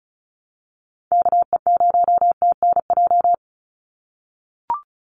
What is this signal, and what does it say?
Morse code, heard as a keyed tone, about 34 wpm, 700 Hz. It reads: KE0TNJ
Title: USA Call Signs 35wpm